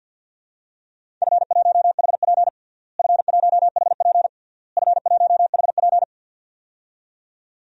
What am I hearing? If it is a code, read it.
Morse code, heard as a keyed tone, about 47 wpm, 700 Hz. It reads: F1HP F1HP F1HP
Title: DXCC-100 Call Signs - 4 Characters - Repeated 3x - 50wpm